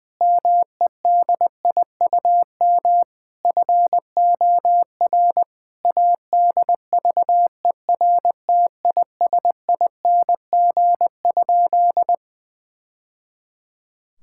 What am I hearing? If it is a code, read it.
Morse code, heard as a keyed tone, about 20 wpm, 700 Hz. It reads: MEDIUM FOR ADVERTISING?
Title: Sentences from Top 1000 Words - Review Part 6 - 20wpm